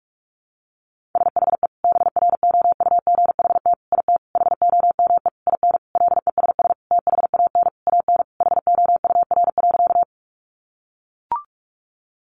Morse code, heard as a keyed tone, about 44 wpm, 700 Hz. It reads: SHE BROUGHT IT HOME IN LESS THAN AN HOUR.
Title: Sentences from Top 500 Words 45wpm